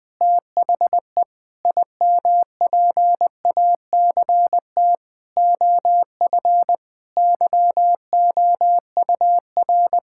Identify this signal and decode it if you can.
Morse code, heard as a keyed tone, about 20 wpm, 700 Hz. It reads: THE IMPACT OF YOUR
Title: Sentences from Top 1000 Words - Review Part 6 - 20wpm